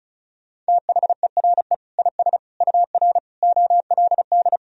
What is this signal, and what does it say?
Morse code, heard as a keyed tone, about 36 wpm, 700 Hz. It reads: THERE IS UR OLD
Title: Sentences from Top 500 Words and Abbreviations 35wpm